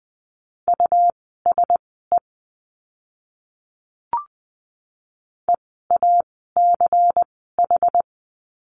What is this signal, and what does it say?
Morse code, heard as a keyed tone, about 20 wpm, spaced as for 15 wpm, 700 Hz. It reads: USE EACH
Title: 192 Lesson 31f - Letter v - Words 20wpm